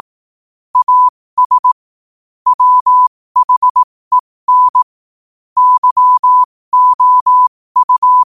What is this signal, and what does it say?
Morse code, heard as a keyed tone, about 18 wpm, spaced as for 15 wpm, 1000 Hz. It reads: AS WHEN YOU
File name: qotdfiles / 15wpm